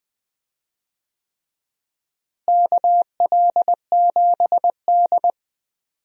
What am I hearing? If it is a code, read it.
Morse code, heard as a keyed tone, about 20 wpm, 700 Hz. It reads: KL7D
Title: DXCC-100 Call Signs - 4 Characters - Repeated 3x - 20wpm